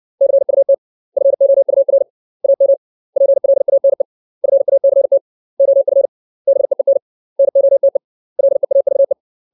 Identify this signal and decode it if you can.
Morse code, heard as a keyed tone, about 60 wpm, 550 Hz. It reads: CUT FOUR AM PLANE FACT OF BEEN NOTE LEAVE
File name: 632 words / 632 Words 60WPM